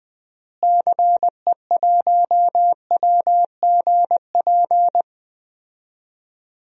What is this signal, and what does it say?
Morse code, heard as a keyed tone, about 20 wpm, 700 Hz. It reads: CE1WGP
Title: DXCC-100 Call Signs - 6 Characters - Repeated 3x - 20wpm